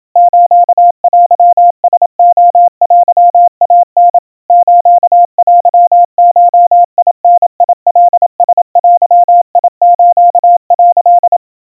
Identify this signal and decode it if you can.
Morse code (Japanese Wabun code), heard as a keyed tone, about 28 wpm, 700 Hz. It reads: ステラレテイタ ステゴダカラデス」
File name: kawa / Morse sample / kamyu 100 cpm 700 Hz Crispy